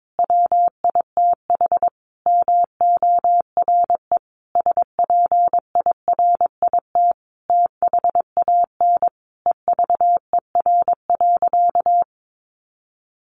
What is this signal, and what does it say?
Morse code, heard as a keyed tone, about 22 wpm, 700 Hz. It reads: WITH MORE SPIRIT THAN EVER.
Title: Sentences from Top 1400 Words 22wpm